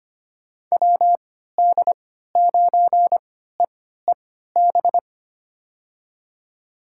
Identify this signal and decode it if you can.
Morse code, heard as a keyed tone, about 25 wpm, spaced as for 14 wpm, 700 Hz. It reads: WD9EEB
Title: USA Call Signs - 6 Characters - 3x Character Spacing - 25wpm